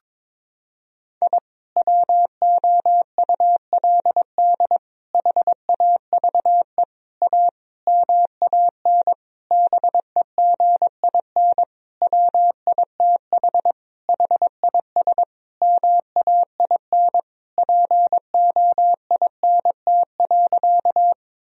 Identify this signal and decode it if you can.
Morse code, heard as a keyed tone, about 22 wpm, 700 Hz. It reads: I WOULD HAVE A MAN BEGIN WITH HIS MAIN POINT.
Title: Sentences from Top 1000 Words - Review Part 2 - 22wpm